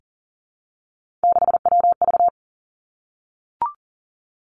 Morse code, heard as a keyed tone, about 40 wpm, 700 Hz. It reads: 6WV
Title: Three Challenging Letter-Number Combinations 40wpm